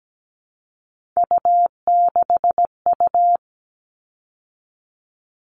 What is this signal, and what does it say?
Morse code, heard as a keyed tone, about 17 wpm, 700 Hz. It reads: U6U
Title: All Cut Numbers - 3-Digit 17wpm